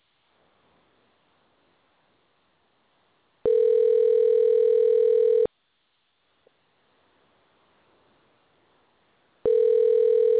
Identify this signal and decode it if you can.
Ringback tone (North America), 440 + 480 Hz.